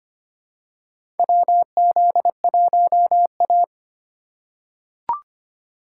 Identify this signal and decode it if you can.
Morse code, heard as a keyed tone, about 25 wpm, 690 Hz. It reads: WZ1A
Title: USA Call Signs - 4 Characters - 4x Character Spacing - 25wpm